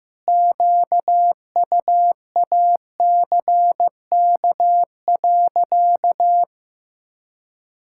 Morse code, heard as a keyed tone, about 15 wpm, 700 Hz. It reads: QUACK.
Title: Pangram Challenge v2 - 15wpm